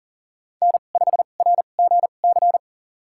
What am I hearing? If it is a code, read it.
Morse code, heard as a keyed tone, about 40 wpm, spaced as for 29 wpm, 700 Hz. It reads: N5RGC
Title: USA Call Signs - 5 Characters - 2x Character Spacing - 40wpm